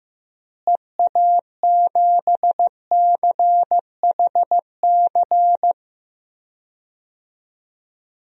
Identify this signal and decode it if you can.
Morse code, heard as a keyed tone, about 15 wpm, 700 Hz. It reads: EA7CHC
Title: DXCC-100 Call Signs - 6 Characters - Repeated 3x - 15wpm